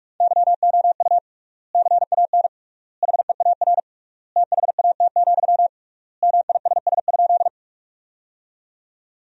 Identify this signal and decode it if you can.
Morse code, heard as a keyed tone, about 46 wpm, 700 Hz. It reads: YOU CAN HEAR THAT, MISS?
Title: 3x Word-Spacing - Sentences from Top 500 Words - 45wpm